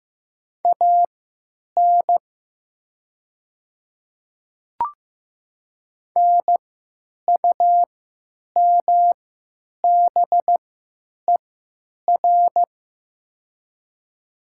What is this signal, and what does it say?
Morse code, heard as a keyed tone, about 15 wpm, spaced as for 9 wpm, 700 Hz. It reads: AN NUMBER
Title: 3x Character Spacing - Top 100 Words - 15wpm